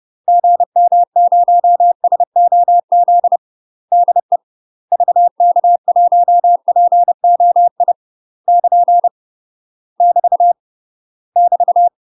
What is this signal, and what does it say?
Morse code, heard as a keyed tone, about 30 wpm, 700 Hz. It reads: GM0SOZ DE VK1POI <KN> <BT> <BT>
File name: UK MASTER QSOs TRAINING / 30WPM / UK MASTER QSOs 0003